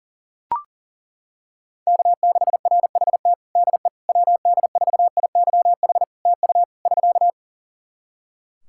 Morse code, heard as a keyed tone, about 42 wpm, 700 Hz. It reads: K6RHT DE WD4IYH TU <SK>
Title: Practice QSO - Exchange 4 - Ending - 40wpm